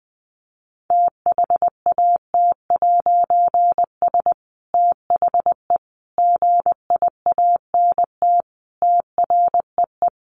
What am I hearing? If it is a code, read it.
Morse code, heard as a keyed tone, about 20 wpm, 700 Hz. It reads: THAT'S THE GIANT TREE
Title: A Sound Of Thunder 20wpm